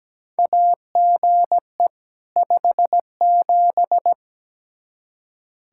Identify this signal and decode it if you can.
Morse code, heard as a keyed tone, about 17 wpm, 700 Hz. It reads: AGE 57
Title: Practice QSO - Exchange 3 - More Info - 17wpm